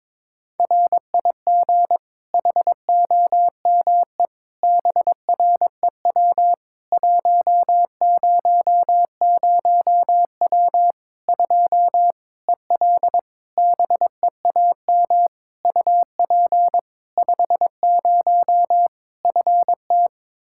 Morse code, heard as a keyed tone, about 22 wpm, 700 Hz. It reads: RIG HOME BREW 100W 2 EL BEAM UP 50 FT
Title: Practice QSO - Exchange 2 - Info - 22wpm